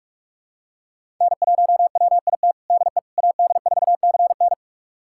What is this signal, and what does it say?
Morse code, heard as a keyed tone, about 45 wpm, 700 Hz. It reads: N1WIT DE AD4CN
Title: USA Callsign DE Callsign - 5 Characters - 45wpm